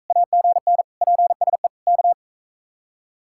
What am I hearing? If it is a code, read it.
Morse code, heard as a keyed tone, about 43 wpm, 700 Hz. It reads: AGN PSE K